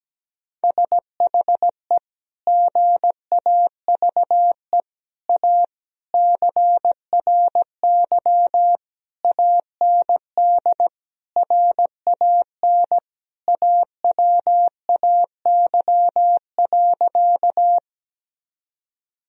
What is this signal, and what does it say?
Morse code, heard as a keyed tone, about 17 wpm, 700 Hz. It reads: SHE GAVE A CRY AND RAN AWAY.
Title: Sentences from Top 1000 Words - Review Part 3 - 17wpm